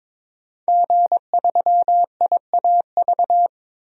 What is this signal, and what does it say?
Morse code, heard as a keyed tone, about 22 wpm, 700 Hz. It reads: G3IAV
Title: DXCC-100 Call Signs - 5 Characters - 22wpm